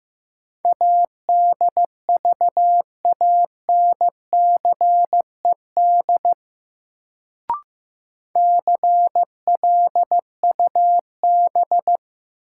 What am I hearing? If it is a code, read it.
Morse code, heard as a keyed tone, about 15 wpm, 700 Hz. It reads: ADVANCED CLUB
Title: Sentences from Top 1200 Words - New Word Drill Alt - 15wpm